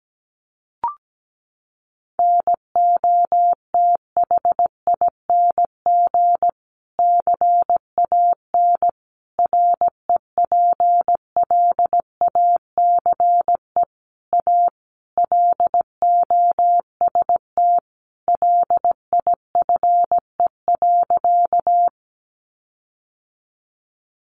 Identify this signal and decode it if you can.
Morse code, heard as a keyed tone, about 17 wpm, 700 Hz. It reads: NOTHING CAN REPLACE A LOST LIFE.
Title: Sentences from Top 1400 Words 17wpm